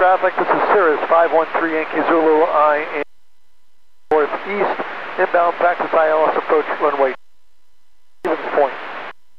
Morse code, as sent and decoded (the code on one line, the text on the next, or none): none